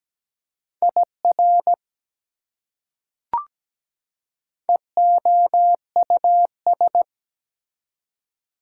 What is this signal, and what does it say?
Morse code, heard as a keyed tone, about 17 wpm, 700 Hz. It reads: IR EOUS
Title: Prefixes and Suffixes 17wpm